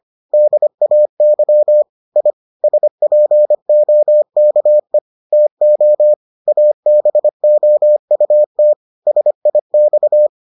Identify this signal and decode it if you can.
Morse code, heard as a keyed tone, about 25 wpm, 600 Hz. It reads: DAY I SPOKE TO ABOUT SIX